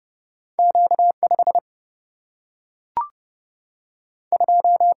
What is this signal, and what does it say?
Morse code, heard as a keyed tone, about 30 wpm, 700 Hz. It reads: Q 5 2
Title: Single-Letter Single-Number 30wpm